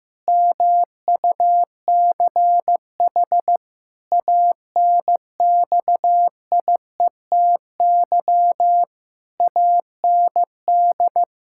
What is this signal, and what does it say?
Morse code, heard as a keyed tone, about 15 wpm, 700 Hz. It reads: MUCH ANXIETY AND
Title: Pangram Challenge v2 - 15wpm